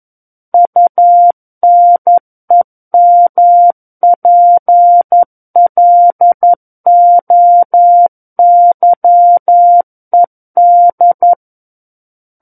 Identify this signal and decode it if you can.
Morse code, heard as a keyed tone, about 11 wpm, 700 Hz. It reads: UNEMPLOYED
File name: 11 WPM